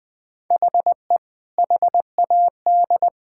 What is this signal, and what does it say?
Morse code, heard as a keyed tone, about 21 wpm, 700 Hz. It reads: HE HAD
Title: Sentences with 2-Word Phrases - Spoken Beginning - 20wpm